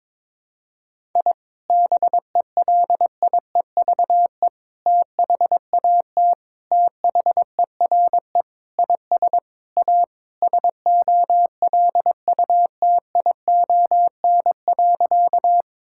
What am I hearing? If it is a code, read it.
Morse code, heard as a keyed tone, about 22 wpm, 700 Hz. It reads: I BELIEVE THAT THERE IS A SOLUTION.
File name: practice / 22wpm / Sentences with 4-Word Phrases - 22wpm